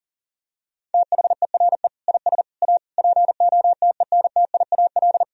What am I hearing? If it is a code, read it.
Morse code, heard as a keyed tone, about 40 wpm, 700 Hz. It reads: THERE IS A POTENTIAL
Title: Sentences from Top 1000 Words - Review Part 4 - 40wpm